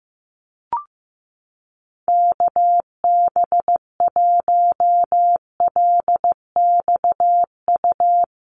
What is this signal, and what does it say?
Morse code, heard as a keyed tone, about 15 wpm, 700 Hz. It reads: KB1LXU
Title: USA Call Signs Repeated 3x 15wpm